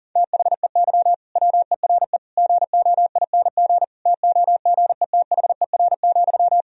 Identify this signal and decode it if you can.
Morse code, heard as a keyed tone, about 37 wpm, 700 Hz. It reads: THEY WERE GOING TOGETHER,
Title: Sentences from Top 1000 Words - Review Part 4 - 40wpm